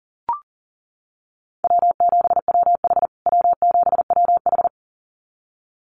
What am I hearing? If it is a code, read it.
Morse code, heard as a keyed tone, about 40 wpm, 700 Hz. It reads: W7WH W7WH
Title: USA Call Signs Repeated 2x 40wpm